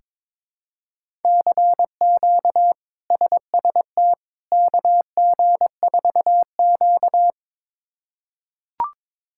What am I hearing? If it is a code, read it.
Morse code, heard as a keyed tone, about 22 wpm, 700 Hz. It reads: CQ SST KG4Q
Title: K1USN SST QSO Practice - 22wpm